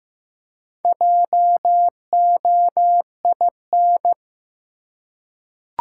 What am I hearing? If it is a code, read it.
Morse code, heard as a keyed tone, about 15 wpm, 700 Hz. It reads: JOIN
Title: Four Letter Words - 15wpm